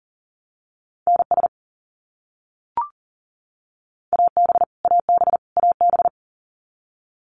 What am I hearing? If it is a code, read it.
Morse code, heard as a keyed tone, about 40 wpm, 700 Hz. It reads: NS AB AB AB
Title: Canadian Provinces and Territories - Repeated 3x - 40wpm